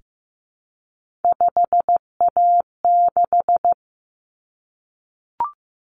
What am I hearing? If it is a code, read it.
Morse code, heard as a keyed tone, about 15 wpm, 700 Hz. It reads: 5A6
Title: Common Cut Numbers - 3-Digit 15wpm